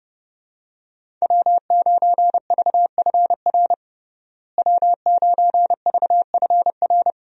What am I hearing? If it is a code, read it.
Morse code, heard as a keyed tone, about 30 wpm, 700 Hz. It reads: W9VFR W9VFR
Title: USA Call Signs - 5 Characters - Repeated 3x - 30wpm